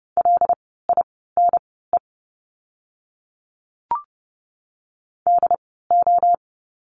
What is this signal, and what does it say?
Morse code, heard as a keyed tone, about 30 wpm, spaced as for 17 wpm, 700 Hz. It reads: LINE DO
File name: course / 30:10wpm / 115 Lesson 19f - Letter w - Words 30wpm